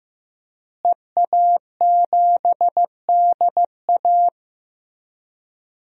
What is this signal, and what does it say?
Morse code, heard as a keyed tone, about 15 wpm, 700 Hz. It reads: EA7DA
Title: DXCC-100 Call Signs - 5 Characters - Repeated 3x - 15wpm